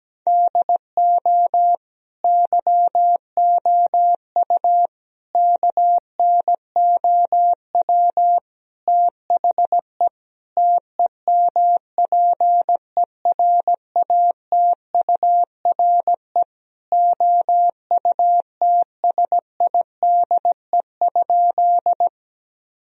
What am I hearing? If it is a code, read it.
Morse code, heard as a keyed tone, about 17 wpm, 700 Hz. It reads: DO YOU KNOW THE TEMPERATURE OUTSIDE?
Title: Sentences from Top 1000 Words - Review Part 4 - 17wpm